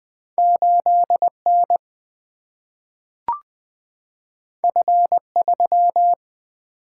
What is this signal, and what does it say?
Morse code, heard as a keyed tone, about 20 wpm, 700 Hz. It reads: 8N F3
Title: Single-Letter Single-Number 20wpm